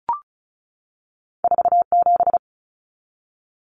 Morse code, heard as a keyed tone, about 36 wpm, 700 Hz. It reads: 47
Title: Numbers - 2-Digit 35wpm